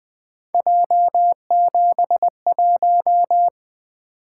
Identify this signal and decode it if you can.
Morse code, heard as a keyed tone, about 20 wpm, 700 Hz. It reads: J71
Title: Three Letter-Number Combinations - 20wpm